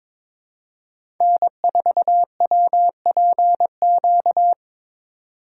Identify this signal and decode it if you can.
Morse code, heard as a keyed tone, about 22 wpm, 700 Hz. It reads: N4WPQ
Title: USA Call Signs - 5 Characters - 2x Character Spacing - 22wpm